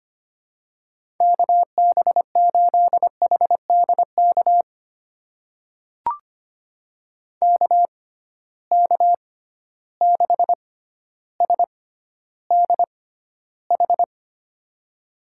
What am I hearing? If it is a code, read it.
Morse code, heard as a keyed tone, about 25 wpm, spaced as for 13 wpm, 700 Hz. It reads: KB8HDK KK6SDH
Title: USA Call Signs - 6x Character Spacing - 25wpm